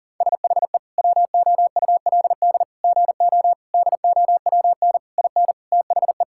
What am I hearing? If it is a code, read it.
Morse code, heard as a keyed tone, about 40 wpm, 700 Hz. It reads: SHE WOULD GO DOWN IN THE
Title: Sentences from Top 1600 Words 40wpm